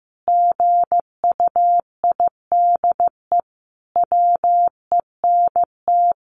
Morse code, heard as a keyed tone, about 15 wpm, 700 Hz. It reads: GUIDE WENT
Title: Sentences from Top 600 Words 15wpm